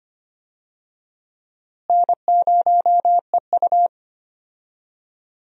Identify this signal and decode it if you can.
Morse code, heard as a keyed tone, about 25 wpm, 700 Hz. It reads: N0EU
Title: USA Call Signs - 4 Characters - Repeated 3x - 25wpm